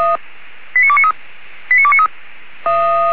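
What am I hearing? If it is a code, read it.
Five-tone selective call (ZVEI), heard as quick runs of five tones, tones of about 70 ms. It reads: 89293 89293
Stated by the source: ZVEI five-tone selective call 89300 received on an SDR.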